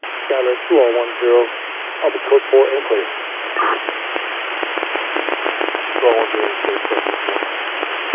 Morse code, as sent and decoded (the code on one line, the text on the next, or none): none